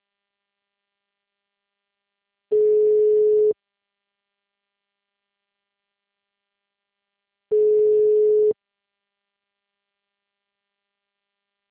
Ringback tone (Europe, 425 Hz).
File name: out- -2030- - - .1265